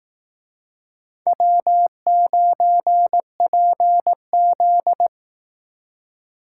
Morse code, heard as a keyed tone, about 18 wpm, 700 Hz. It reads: W9PZ
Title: Speed Racing - USA 1x2 Call Signs 15wpm